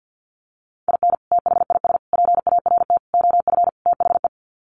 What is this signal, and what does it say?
Morse code, heard as a keyed tone, about 50 wpm, 700 Hz. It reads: IN THIS PART OF THE
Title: Sets of 5 Words 50wpm